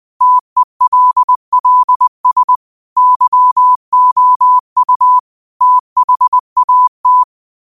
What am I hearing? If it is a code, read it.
Morse code, heard as a keyed tone, about 20 wpm, 1000 Hz. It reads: TELLS YOU THAT
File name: qotdfiles / 20wpm